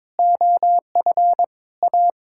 Morse code, heard as a keyed tone, about 22 wpm, 700 Hz. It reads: OF A
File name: practice / 22wpm / Sets of 8 Words - 22wpm